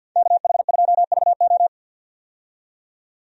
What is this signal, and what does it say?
Morse code, heard as a keyed tone, about 48 wpm, 700 Hz. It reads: KH2VO
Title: USA Call Signs - 5 Characters - 50wpm